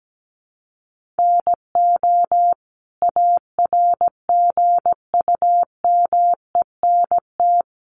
Morse code, heard as a keyed tone, about 17 wpm, 700 Hz. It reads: NO ARGUMENT
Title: Sentences from Top 1400 Words 17wpm